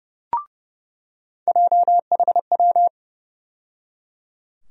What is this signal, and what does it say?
Morse code, heard as a keyed tone, about 30 wpm, 700 Hz. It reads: JHW
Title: USA Call Sign Postfixes - 3 Letters 30wpm